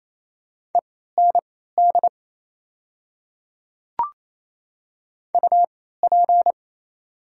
Morse code, heard as a keyed tone, about 27 wpm, spaced as for 16 wpm, 700 Hz. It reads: END UP
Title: 3x Character Spacing - Top 200 Words - 28wpm